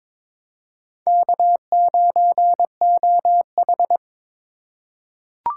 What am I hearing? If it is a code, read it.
Morse code, heard as a keyed tone, about 22 wpm, 700 Hz. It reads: K9OH
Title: USA Call Signs - 4 Characters - 8x Character Spacing - 22wpm